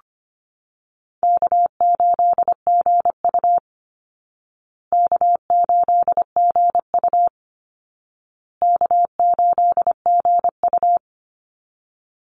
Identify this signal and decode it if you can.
Morse code, heard as a keyed tone, about 25 wpm, 700 Hz. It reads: K8GU K8GU K8GU
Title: Easy Call Signs Repeated 3x - 25wpm